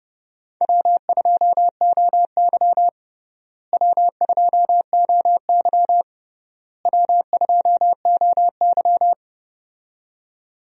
Morse code, heard as a keyed tone, about 30 wpm, 700 Hz. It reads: W2OY W2OY W2OY
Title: USA Call Signs - 4 Characters - Repeated 3x - 30wpm